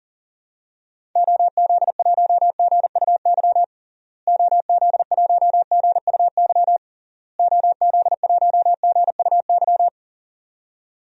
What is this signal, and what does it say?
Morse code, heard as a keyed tone, about 40 wpm, 700 Hz. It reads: OZ1GUY OZ1GUY OZ1GUY
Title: DXCC-100 Call Signs - 6 Characters - Repeated 3x - 40wpm